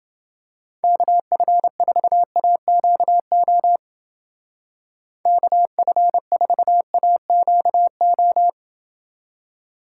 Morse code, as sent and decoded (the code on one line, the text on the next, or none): -.- ..-. ....- .- --.- --- / -.- ..-. ....- .- --.- ---
KF4AQO KF4AQO